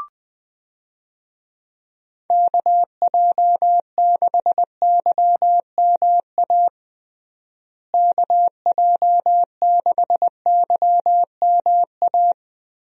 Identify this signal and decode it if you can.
Morse code, heard as a keyed tone, about 20 wpm, 700 Hz. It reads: KJ6YMA KJ6YMA
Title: USA Call Signs - 6 Characters - Repeated 3x - 20wpm